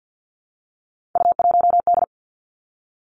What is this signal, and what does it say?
Morse code, heard as a keyed tone, about 50 wpm, 700 Hz. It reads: U1D
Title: Three Challenging Letter-Number Combinations 50wpm